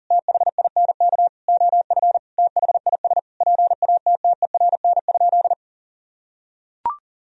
Morse code, heard as a keyed tone, about 37 wpm, 700 Hz. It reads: THINK OF THIS PATTERN?
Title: Sentences from Top 1000 Words - Review Part 2 - 40wpm